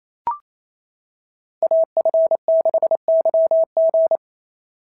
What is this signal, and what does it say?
Morse code, heard as a keyed tone, about 29 wpm, 690 Hz. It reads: AF6YG
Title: SOTA QSO Practice 28wpm